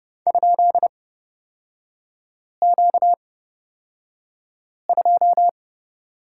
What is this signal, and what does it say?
Morse code, heard as a keyed tone, about 29 wpm, spaced as for 5 wpm, 690 Hz. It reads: ?Q2
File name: Lesson 38e - Letter q - Three Characters 30wpm